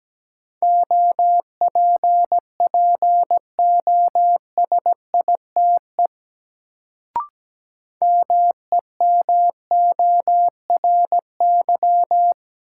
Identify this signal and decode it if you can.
Morse code, heard as a keyed tone, about 17 wpm, 700 Hz. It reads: OPPOSITE MEMORY
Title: Sentences from Top 1300 Words - New Word Drill - 17wpm